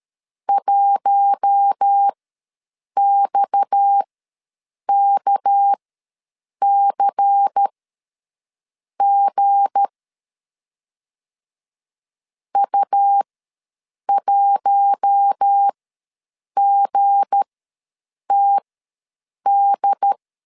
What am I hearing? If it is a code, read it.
Morse code, heard as a keyed tone, about 13 wpm, spaced as for 7 wpm, 800 Hz. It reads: 1XKCG U1GTD
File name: LEZIONI / 7 GF9Z / 7 G ws 12wpm 3140